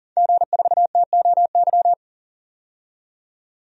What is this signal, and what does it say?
Morse code, heard as a keyed tone, about 40 wpm, 700 Hz. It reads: G4TOY
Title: DXCC-100 Call Signs - 5 Characters - 40wpm